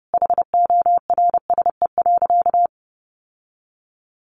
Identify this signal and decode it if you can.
Morse code, heard as a keyed tone, about 30 wpm, 700 Hz. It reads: HORSE.
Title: Sentences from Top 1100 Words 30wpm